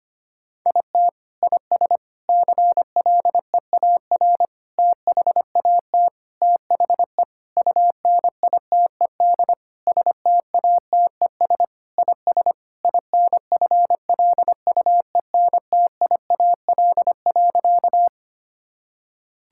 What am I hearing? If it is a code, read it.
Morse code, heard as a keyed tone, about 25 wpm, 700 Hz. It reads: IT IS CLEAR THAT THE UNITED STATES IS INFLUENTIAL.
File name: practice / 25wpm / Sentences with 4-Word Phrases - 25wpm